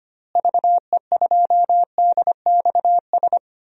Morse code, heard as a keyed tone, about 25 wpm, 700 Hz. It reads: VE2DXS
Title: DXCC-100 Call Signs - 6 Characters - Repeated 3x - 25wpm